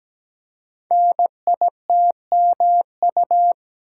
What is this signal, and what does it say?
Morse code, heard as a keyed tone, about 17 wpm, 700 Hz. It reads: NITMU
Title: Code Groups - 5 Letter:Numbers - 17wpm